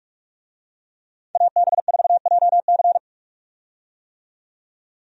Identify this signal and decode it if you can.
Morse code, heard as a keyed tone, about 44 wpm, 700 Hz. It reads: AB4JC
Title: USA Call Signs - 5 Characters - 45wpm